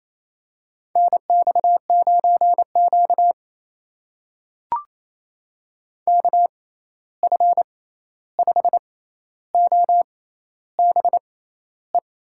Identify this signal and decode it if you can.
Morse code, heard as a keyed tone, about 28 wpm, spaced as for 13 wpm, 700 Hz. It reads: NX9Q KF5OBE
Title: USA Call Signs - 6x Character Spacing - 28wpm